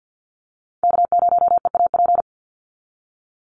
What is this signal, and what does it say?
Morse code, heard as a keyed tone, about 50 wpm, 700 Hz. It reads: K0EAP
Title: NA SOTA Chaser Callsigns 50wpm